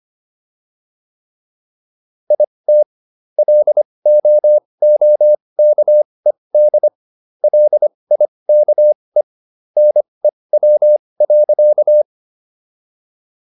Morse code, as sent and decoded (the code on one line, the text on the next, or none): .. - / .-.. --- --- -.- . -.. / .-.. .. -.- . / -. .-- .-.-.-
IT LOOKED LIKE NEW.